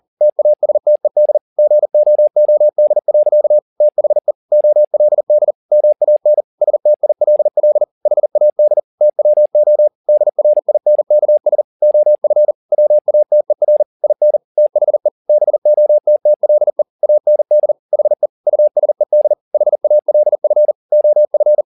Morse code, heard as a keyed tone, about 40 wpm, 600 Hz. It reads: TASTED GOOD. THE OLD MAN STILL HAD TWO DRINKS OF WATER IN THE BOTTLE AND HE USED HALF OF
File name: the old man and the sea 2 40wpm